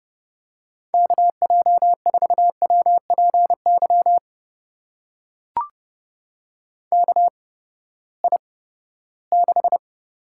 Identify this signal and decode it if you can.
Morse code, heard as a keyed tone, about 30 wpm, spaced as for 17 wpm, 700 Hz. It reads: KJ4WPY KI6
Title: USA Call Signs - 6 Characters - 8x Character Spacing - 30wpm